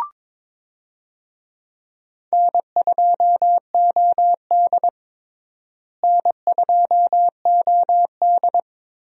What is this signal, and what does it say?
Morse code, heard as a keyed tone, about 23 wpm, 700 Hz. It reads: N2OD N2OD
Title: USA Call Signs - 4 Characters - Repeated 3x - 22wpm